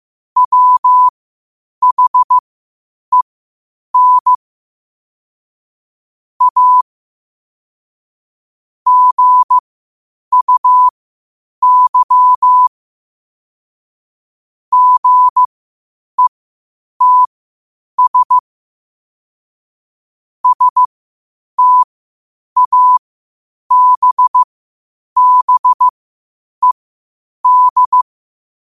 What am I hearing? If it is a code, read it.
Morse code, heard as a keyed tone, about 15 wpm, spaced as for 8 wpm, 1000 Hz. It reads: WHEN A GUY GETS STABBED
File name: qotdfiles / 8wpm